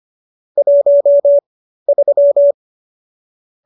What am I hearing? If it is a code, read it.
Morse code, heard as a keyed tone, about 25 wpm, spaced as for 13 wpm, 570 Hz. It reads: 13